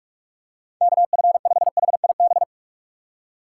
Morse code, heard as a keyed tone, about 46 wpm, 700 Hz. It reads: KF5HIB
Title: USA Call Signs - 6 Characters - 2x Character Spacing - 45wpm